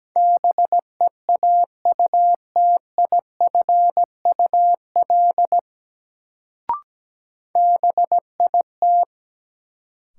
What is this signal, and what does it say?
Morse code, heard as a keyed tone, about 17 wpm, 700 Hz. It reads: BEAUTIFUL BIT
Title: Sentences from Top 900 Words - New Word Drill - 17wpm